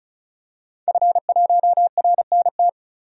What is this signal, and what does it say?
Morse code, heard as a keyed tone, about 36 wpm, 700 Hz. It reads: F1RNT
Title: DXCC-100 Call Signs - 5 Characters - 35wpm